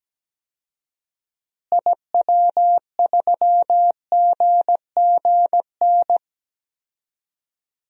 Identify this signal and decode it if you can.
Morse code, heard as a keyed tone, about 17 wpm, 700 Hz. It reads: IW3GGN
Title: DXCC-100 Call Signs - 6 Characters - Repeated 2x - 17wpm